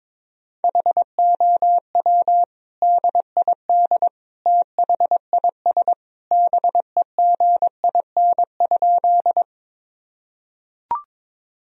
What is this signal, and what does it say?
Morse code, heard as a keyed tone, about 22 wpm, 700 Hz. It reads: HOW DID THIS BEGIN?
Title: Sentences from Top 1000 Words - Review Part 1 - 22wpm